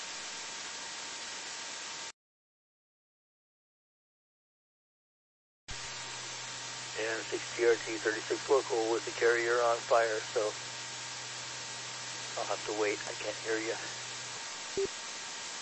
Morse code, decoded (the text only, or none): none